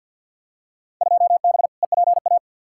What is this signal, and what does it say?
Morse code, heard as a keyed tone, about 45 wpm, 700 Hz. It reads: 2B EPA
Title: Field Day Exchanges Encore - 50wpm